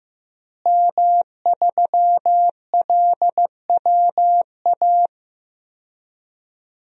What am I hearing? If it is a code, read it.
Morse code, heard as a keyed tone, about 15 wpm, 700 Hz. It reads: M3LWA
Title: DXCC-100 Call Signs - 5 Characters - Repeated 3x - 15wpm